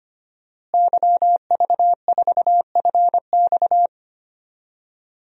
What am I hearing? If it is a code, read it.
Morse code, heard as a keyed tone, about 25 wpm, 700 Hz. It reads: YV4FX
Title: DXCC-100 Call Signs - 5 Characters - Repeated 3x - 25wpm